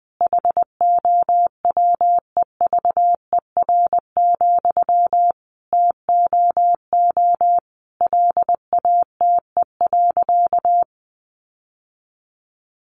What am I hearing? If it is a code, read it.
Morse code, heard as a keyed tone, about 20 wpm, 700 Hz. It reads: HOWEVER, TOO LATE.
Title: Sentences from Top 600 Words 20wpm